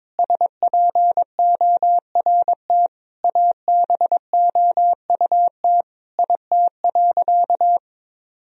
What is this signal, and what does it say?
Morse code, heard as a keyed tone, about 22 wpm, 700 Hz. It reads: SPORT ABOUT IT.
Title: Sentences from Top 1000 Words - Review Part 4 - 22wpm